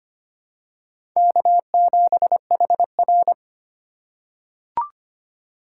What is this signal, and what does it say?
Morse code, heard as a keyed tone, about 25 wpm, 700 Hz. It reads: K7HR
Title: USA Call Signs - 4 Characters - 3x Character Spacing - 25wpm